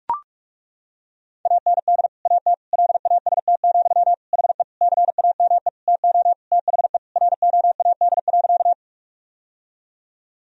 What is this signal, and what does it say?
Morse code, heard as a keyed tone, about 41 wpm, 700 Hz. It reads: AND AT LAST, HE CAME TO THE ROAD.
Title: Sentences from Top 1000 Words - Review Part 2 - 45wpm